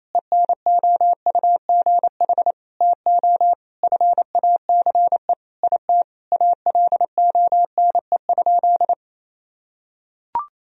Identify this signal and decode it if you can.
Morse code, heard as a keyed tone, about 28 wpm, 700 Hz. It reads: ENOUGH TO FACE IT ALONE?
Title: Sentences from Top 1000 Words - Review Part 3 - 28wpm